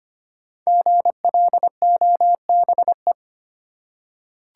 Morse code, heard as a keyed tone, about 25 wpm, 700 Hz. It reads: GLOBE
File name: 5 Letter Words - First 3 Letters Given - 25wpm